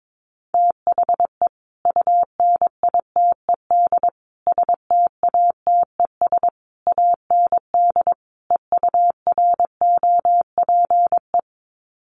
Morse code, heard as a keyed tone, about 22 wpm, 700 Hz. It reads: THE UNITED STATES AND EUROPE
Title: Sets of 5 Words Encore 22wpm